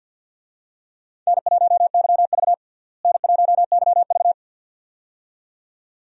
Morse code, heard as a keyed tone, about 47 wpm, 700 Hz. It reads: N1YV N1YV
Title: USA Call Signs - 4 Characters - Repeated 2x - 50wpm